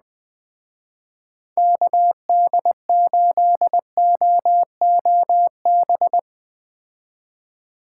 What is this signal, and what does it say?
Morse code, heard as a keyed tone, about 20 wpm, 700 Hz. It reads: KD8OOB
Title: Speed Racing - USA 2x3 Call Signs 17wpm